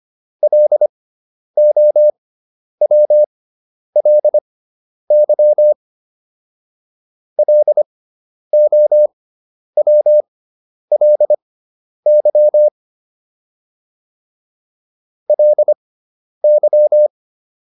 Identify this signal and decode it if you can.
Morse code, heard as a keyed tone, about 25 wpm, spaced as for 10 wpm, 600 Hz. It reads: LOWLY LOWLY LY